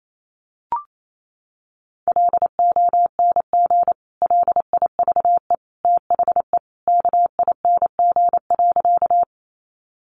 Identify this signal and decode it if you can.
Morse code, heard as a keyed tone, about 28 wpm, 700 Hz. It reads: LONG LIVE THE KING.
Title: Sentences from Top 400 Words 28wpm